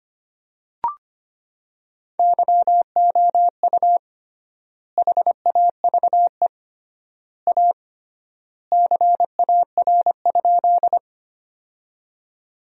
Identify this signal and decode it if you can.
Morse code, heard as a keyed tone, about 25 wpm, 700 Hz. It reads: YOU HAVE A CAR?
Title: 3x Word-Spacing - Sentences from Top 300 Words - 25wpm